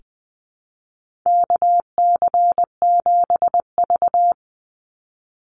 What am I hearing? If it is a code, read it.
Morse code, heard as a keyed tone, about 20 wpm, 700 Hz. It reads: KC7V
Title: Call Signs - 20wpm